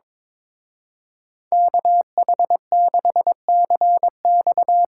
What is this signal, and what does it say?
Morse code, heard as a keyed tone, about 23 wpm, 700 Hz. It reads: KH6CX
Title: Speed Racing - USA 2x2 Call Signs 22wpm